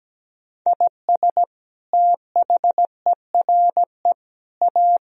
Morse code, heard as a keyed tone, about 17 wpm, 700 Hz. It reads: IS THERE A
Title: Sentences from Top 1000 Words - Review Part 3 - 17wpm